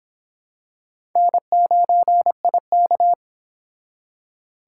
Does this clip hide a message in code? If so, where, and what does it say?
Yes: Morse code from 1.2 to 3.1 s, heard as a keyed tone, about 25 wpm, 700 Hz. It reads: N9IK